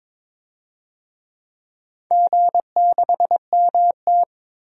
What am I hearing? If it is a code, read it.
Morse code, heard as a keyed tone, about 22 wpm, 700 Hz. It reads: G6MT
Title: DXCC-100 Call Signs - 4 Characters - Repeated 3x - 22wpm